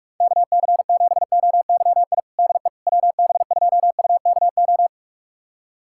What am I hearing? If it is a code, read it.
Morse code, heard as a keyed tone, about 41 wpm, 700 Hz. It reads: KC7OYI DE WB2UKO
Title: USA Callsign DE Callsign - 6 Characters - 45wpm